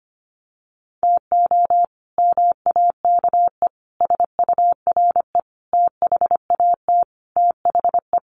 Morse code, heard as a keyed tone, about 25 wpm, 700 Hz. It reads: TO MAKE SURE THAT THE
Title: Sets of 5 Words 25wpm